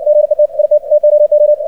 Morse code, heard as a keyed tone, about 60 wpm, 600 Hz. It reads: GET IT TOO
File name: archived / CRSnet